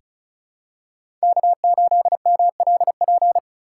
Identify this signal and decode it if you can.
Morse code, heard as a keyed tone, about 35 wpm, 700 Hz. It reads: K8MLP
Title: USA Call Signs - 5 Characters - 2x Character Spacing - 35wpm